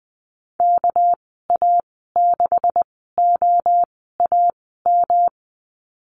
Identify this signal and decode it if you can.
Morse code, heard as a keyed tone, about 20 wpm, spaced as for 15 wpm, 700 Hz. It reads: KA6OAM
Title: 219 Lesson 35g - Number 0 - Call Signs 20wpm